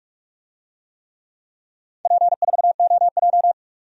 Morse code, heard as a keyed tone, about 46 wpm, 700 Hz. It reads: P4OJ
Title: DXCC-100 Call Signs - 4 Characters - 45wpm